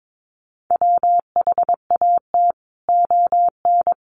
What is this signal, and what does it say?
Morse code, heard as a keyed tone, about 22 wpm, 700 Hz. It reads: WHAT ON
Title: Sentences from Top 200 Words 22wpm